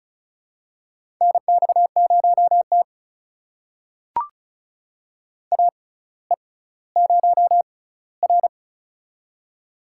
Morse code, heard as a keyed tone, about 36 wpm, spaced as for 18 wpm, 700 Hz. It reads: NX0T AE0R
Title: USA Call Signs - 4 Characters - 6x Character Spacing - 35wpm